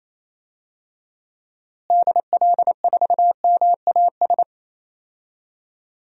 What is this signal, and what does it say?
Morse code, heard as a keyed tone, about 28 wpm, 700 Hz. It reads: DL4MAS
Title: DXCC-100 Call Signs - 6 Characters - Repeated 3x - 28wpm